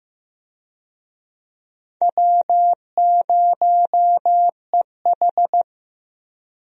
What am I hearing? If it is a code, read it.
Morse code, heard as a keyed tone, about 15 wpm, 700 Hz. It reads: W0EH
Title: USA Call Signs - 4 Characters - Repeated 3x - 15wpm